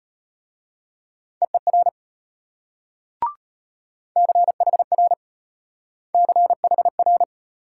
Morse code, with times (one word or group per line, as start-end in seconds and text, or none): 1.4-1.9 EER
4.2-5.1 CHR
6.2-7.2 CHR